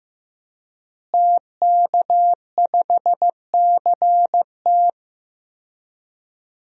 Morse code, heard as a keyed tone, about 15 wpm, 700 Hz. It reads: TK5CT
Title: DXCC-100 Call Signs - 5 Characters - Repeated 2x - 15wpm